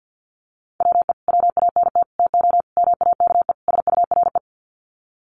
Morse code, heard as a keyed tone, about 50 wpm, 700 Hz. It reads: WE WANT TO MAKE SURE
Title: Sets of 5 Words Encore2 50wpm